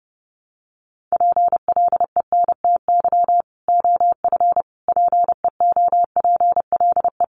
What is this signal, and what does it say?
Morse code, heard as a keyed tone, about 30 wpm, 700 Hz. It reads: PLENTY OF PEOPLE
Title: Sentences from Top 1300 Words 30wpm